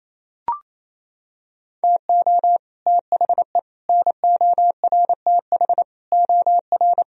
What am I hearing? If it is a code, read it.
Morse code, heard as a keyed tone, about 27 wpm, 700 Hz. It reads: TO THE NORTH OR
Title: Sentences from Top 1000 Words - Review Part 2 - 28wpm